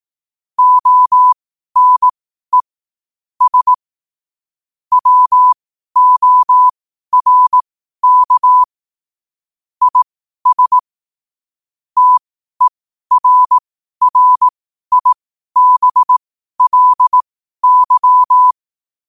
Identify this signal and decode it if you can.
Morse code, heard as a keyed tone, about 18 wpm, spaced as for 12 wpm, 1000 Hz. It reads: ONE S WORK IS TERRIBLY
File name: qotdfiles / 12wpm